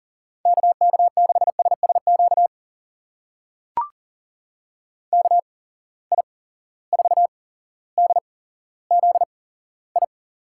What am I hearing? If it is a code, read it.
Morse code, heard as a keyed tone, about 42 wpm, spaced as for 17 wpm, 700 Hz. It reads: KK6SSQ KI4DZI
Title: USA Call Signs - 6 Characters - 8x Character Spacing - 40wpm